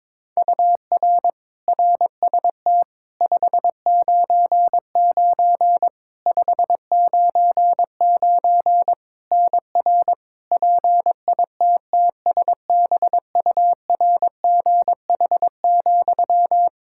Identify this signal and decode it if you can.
Morse code, heard as a keyed tone, about 22 wpm, 700 Hz. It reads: UR RST 599 599 NR PITTSBURGH,
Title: Practice QSO - Exchange 1 - Beginning - 22wpm